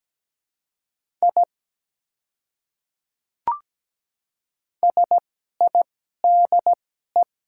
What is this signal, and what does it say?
Morse code, heard as a keyed tone, about 17 wpm, spaced as for 12 wpm, 700 Hz. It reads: I SIDE
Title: 2x Character Spacing - Top 100 Words - 17wpm